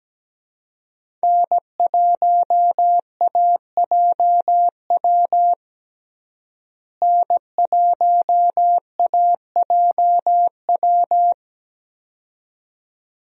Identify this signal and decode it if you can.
Morse code, heard as a keyed tone, about 17 wpm, 700 Hz. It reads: N1AJW N1AJW
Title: USA Call Signs - 5 Characters - Repeated 2x - 17wpm